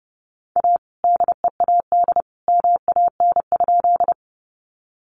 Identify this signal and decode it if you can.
Morse code, heard as a keyed tone, about 30 wpm, 700 Hz. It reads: A DEAD MAN?
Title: Sentences from Top 1500 Words 30wpm